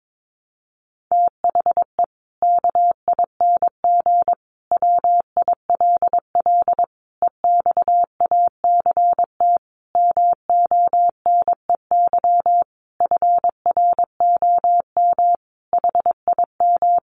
Morse code, heard as a keyed tone, about 22 wpm, 700 Hz. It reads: THE KING WILL EXACT MONEY FROM HIM
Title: Sentences from Top 1300 Words 22wpm